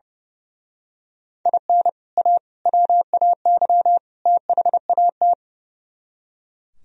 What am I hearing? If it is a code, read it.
Morse code, heard as a keyed tone, about 29 wpm, 700 Hz. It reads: IN A WAY THAT
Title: Speed-Racing - Sets of 4 Words 30wpm